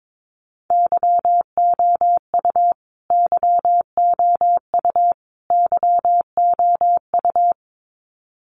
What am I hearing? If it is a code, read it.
Morse code, heard as a keyed tone, about 22 wpm, 700 Hz. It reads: YOU YOU YOU
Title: Top 300 Words Repeated 22wpm